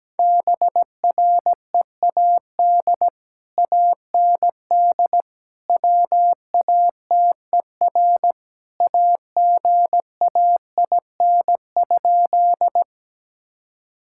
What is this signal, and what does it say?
Morse code, heard as a keyed tone, about 17 wpm, 700 Hz. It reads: BREAD AND WATER AGAIN?
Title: Sentences from Top 1600 Words 17wpm